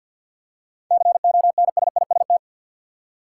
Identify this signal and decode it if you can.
Morse code, heard as a keyed tone, about 53 wpm, 700 Hz. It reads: CONSIST
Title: Seven Letter Words - 50wpm